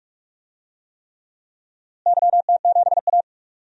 Morse code, heard as a keyed tone, about 46 wpm, 700 Hz. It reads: YT7A
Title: DXCC-100 Call Signs - 4 Characters - Repeated 3x - 45wpm